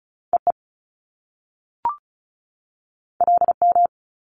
Morse code, heard as a keyed tone, about 36 wpm, 700 Hz. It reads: EE LM
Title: Top Two Letter Combinations in Words 35wpm